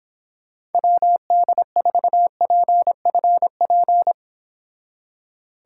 Morse code, heard as a keyed tone, about 29 wpm, 700 Hz. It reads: WD4PFP